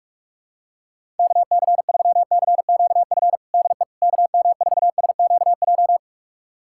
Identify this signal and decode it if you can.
Morse code, heard as a keyed tone, about 46 wpm, 700 Hz. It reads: KC3CQF DE KM4SQJ
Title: USA Callsign DE Callsign - 6 Characters - 45wpm